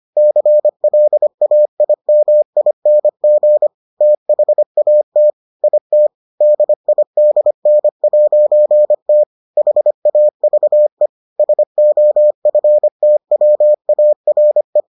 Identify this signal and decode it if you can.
Morse code, heard as a keyed tone, about 25 wpm, 600 Hz. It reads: CLAIMING THAT IT DIDN'T HAVE SOFTWARE